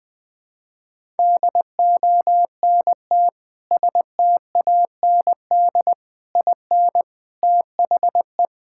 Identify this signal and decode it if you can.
Morse code, heard as a keyed tone, about 21 wpm, 700 Hz. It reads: DONT STAND IN THE
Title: Sentences with 3-Word Phrases - Spoken Beginning - 20wpm